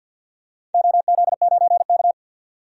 Morse code, heard as a keyed tone, about 46 wpm, 700 Hz. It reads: OZ9K
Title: DXCC-100 Call Signs - 4 Characters - 50wpm